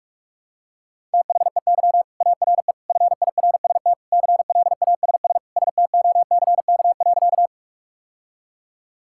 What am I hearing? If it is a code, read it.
Morse code, heard as a keyed tone, about 46 wpm, 700 Hz. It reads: THEY ARE FIRST CLASS STOCK.
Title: Sentences from Top 1000 Words - Review Part 5 - 45wpm